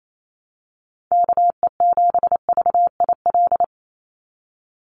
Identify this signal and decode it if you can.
Morse code, heard as a keyed tone, about 28 wpm, 700 Hz. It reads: KE7VIL
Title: USA Call Signs 28wpm